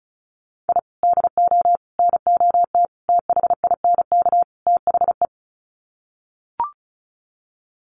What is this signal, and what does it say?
Morse code, heard as a keyed tone, about 34 wpm, 700 Hz. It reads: I DO NOT THINK THE
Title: Sets of 5 Words 35wpm